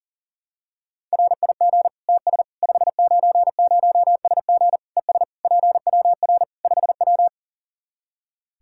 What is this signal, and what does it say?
Morse code, heard as a keyed tone, about 37 wpm, 700 Hz. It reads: RIG TS 590SG ES PWR 5W
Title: Practice QSO - Exchange 2 - Info - 40wpm